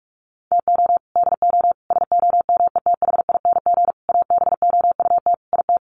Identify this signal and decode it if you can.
Morse code, heard as a keyed tone, about 44 wpm, 700 Hz. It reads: TO DO SOMETHING ABOUT IT
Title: Sets of 5 Words 45wpm